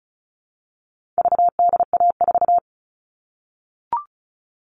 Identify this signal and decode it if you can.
Morse code, heard as a keyed tone, about 34 wpm, 700 Hz. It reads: VDA4
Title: All Cut Numbers - 4-Digit 35wpm